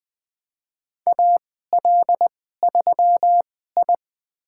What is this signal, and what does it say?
Morse code, heard as a keyed tone, about 21 wpm, spaced as for 15 wpm, 700 Hz. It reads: AL3I
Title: USA Call Signs - 4 Characters - 2x Character Spacing - 20wpm